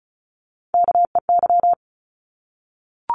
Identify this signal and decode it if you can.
Morse code, heard as a keyed tone, about 34 wpm, 700 Hz. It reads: KEY
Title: Top 501-750 Words 35wpm